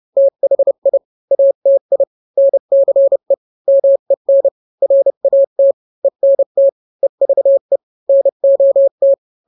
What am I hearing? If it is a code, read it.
Morse code, heard as a keyed tone, about 31 wpm, 550 Hz. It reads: THI ATI NCE MEN RAT ENT EVE NOT